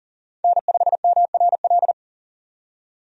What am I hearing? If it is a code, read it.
Morse code, heard as a keyed tone, about 40 wpm, 700 Hz. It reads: N5MRL
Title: USA Call Signs - 5 Characters - 40wpm